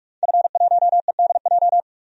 Morse code, heard as a keyed tone, about 46 wpm, 700 Hz. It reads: F1EDJ
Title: DXCC-100 Call Signs - 5 Characters - 45wpm